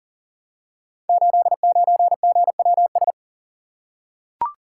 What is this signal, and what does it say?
Morse code, heard as a keyed tone, about 38 wpm, 700 Hz. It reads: 89GWS
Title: Code Groups - 5 Letter:Numbers - 40wpm